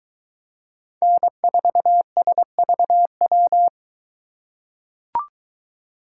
Morse code, heard as a keyed tone, about 22 wpm, 700 Hz. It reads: N4SVW